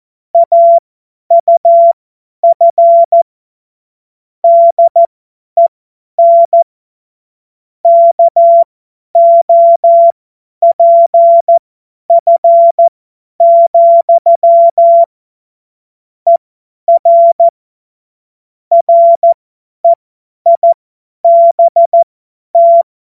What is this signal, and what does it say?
Morse code, heard as a keyed tone, about 14 wpm, spaced as for 10 wpm, 680 Hz. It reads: AUF DEN KOPF, ER REIBT
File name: Morsekurs / Witze / Witz 18. Januar 2022, 50 F70